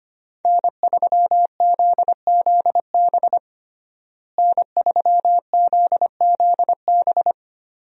Morse code, heard as a keyed tone, about 25 wpm, 700 Hz. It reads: N3ZZB N3ZZB
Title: USA Call Signs - 5 Characters - Repeated 3x - 25wpm